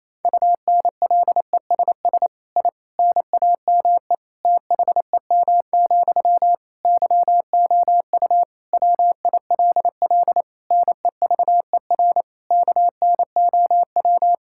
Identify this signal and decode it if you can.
Morse code, heard as a keyed tone, about 28 wpm, 700 Hz. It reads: UNLESS I NAME THEM, YOU WILL NEVER KNOW
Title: Sentences from Top 1000 Words - Review Part 6 - 28wpm